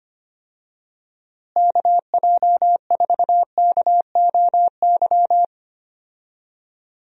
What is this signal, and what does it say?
Morse code, heard as a keyed tone, about 25 wpm, 700 Hz. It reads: KJ4KOY
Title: USA Call Signs - 6 Characters - Repeated 3x - 25wpm